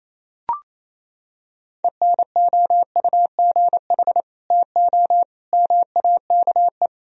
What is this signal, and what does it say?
Morse code, heard as a keyed tone, about 28 wpm, 700 Hz. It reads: ENOUGH TO MAKE
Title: Sentences from Top 1000 Words - Review Part 1 - 28wpm